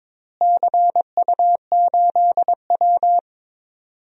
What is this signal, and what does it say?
Morse code, heard as a keyed tone, about 22 wpm, 700 Hz. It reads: CU8W
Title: DXCC-100 Call Signs - 4 Characters - Repeated 3x - 22wpm